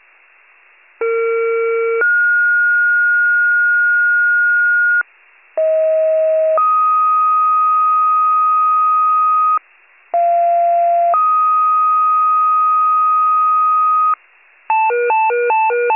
Two-tone page, heard as a one-second tone, then a three-second tone, about 460 then 1430 Hz, about 640 then 1250 Hz, about 690 then 1250 Hz. It is heard from 1.0 to 14.1 s.
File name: pager / 41 FIRE / wcalert